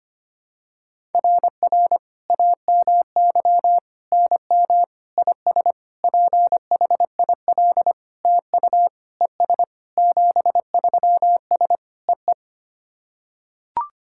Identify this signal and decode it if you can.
Morse code, heard as a keyed tone, about 25 wpm, 700 Hz. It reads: RR AMY NM IS PHIL TU ES 73S EE
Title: SOTA QSO Practice 25wpm